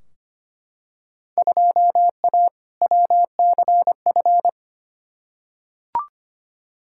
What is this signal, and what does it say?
Morse code, heard as a keyed tone, about 25 wpm, 700 Hz. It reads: 2A WCF
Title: Field Day Exchanges Encore - 25wpm